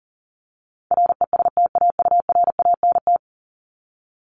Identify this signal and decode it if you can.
Morse code, heard as a keyed tone, about 40 wpm, 700 Hz. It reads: RESTAURANT
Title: Top 1751-2000 Words 40wpm